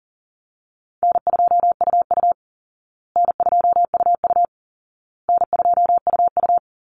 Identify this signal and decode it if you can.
Morse code, heard as a keyed tone, about 40 wpm, 700 Hz. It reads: N2UU N2UU N2UU
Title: Call Signs Repeated 2x - 40wpm